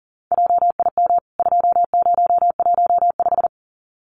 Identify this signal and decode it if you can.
Morse code, heard as a keyed tone, about 39 wpm, 700 Hz. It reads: JIM 2015
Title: CWT Exchange 40wpm